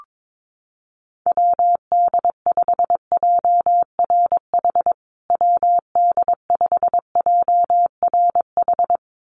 Morse code, heard as a keyed tone, about 22 wpm, 700 Hz. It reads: WD5JRH WD5JRH
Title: USA Call Signs Repeated 2x 22wpm